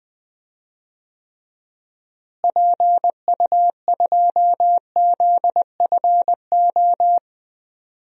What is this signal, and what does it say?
Morse code, heard as a keyed tone, about 20 wpm, 700 Hz. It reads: PU2ZFO
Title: DXCC-100 Call Signs - 6 Characters - Repeated 2x - 20wpm